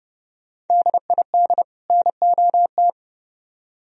Morse code, heard as a keyed tone, about 28 wpm, 700 Hz. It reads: DID NOT